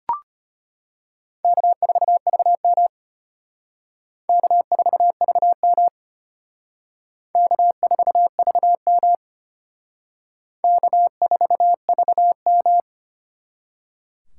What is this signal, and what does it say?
Morse code, heard as a keyed tone, about 31 wpm, 700 Hz. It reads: K4VM K4VM K4VM K4VM
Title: Speed Racing - USA 1x2 Call Signs 25wpm